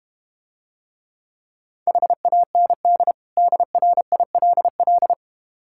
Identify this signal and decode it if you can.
Morse code, heard as a keyed tone, about 32 wpm, 700 Hz. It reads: HAND DRILL